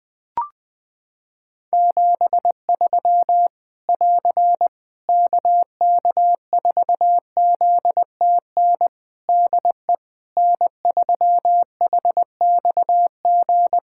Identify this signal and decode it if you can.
Morse code, heard as a keyed tone, about 20 wpm, 700 Hz. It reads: 73 <AR> KK4ZTN DE N3HXG
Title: Practice QSO - Exchange 4 - Ending - 20wpm